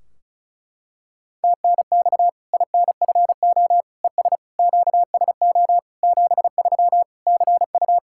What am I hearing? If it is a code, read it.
Morse code, heard as a keyed tone, about 34 wpm, 700 Hz. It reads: TNX INFO ES QSO 73 CU
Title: K1USN SST QSO Practice 35wpm